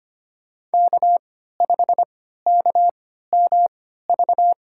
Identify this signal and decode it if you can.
Morse code, heard as a keyed tone, about 25 wpm, spaced as for 14 wpm, 700 Hz. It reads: K5KMV
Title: USA Call Signs - 5 Characters - 3x Character Spacing - 25wpm